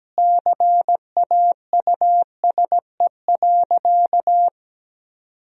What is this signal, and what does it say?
Morse code, heard as a keyed tone, about 17 wpm, 700 Hz. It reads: CAUSE.
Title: Sentences from Top 1000 Words - Review Part 1 - 17wpm